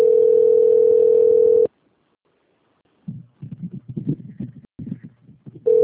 Ringback tone (North America), 440 + 480 Hz.
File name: out- - - - - .17969